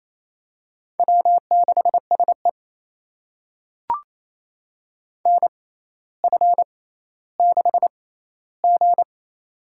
Morse code, heard as a keyed tone, about 28 wpm, spaced as for 14 wpm, 700 Hz. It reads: W6SE NF6G
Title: USA Call Signs - 4 Characters - 6x Character Spacing - 28wpm